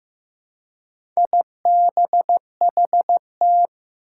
Morse code, heard as a keyed tone, about 15 wpm, 700 Hz. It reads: IBHT
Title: Four Challenging Letter-Number Combinations 15wpm